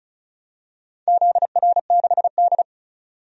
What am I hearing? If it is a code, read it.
Morse code, heard as a keyed tone, about 36 wpm, 700 Hz. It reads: ZR6D
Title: DXCC-100 Call Signs - 4 Characters - 35wpm